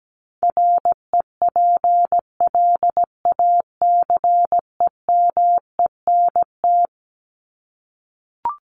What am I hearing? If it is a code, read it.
Morse code, heard as a keyed tone, about 17 wpm, 700 Hz. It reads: REPLACEMENT
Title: Top 1751-2000 Words 17wpm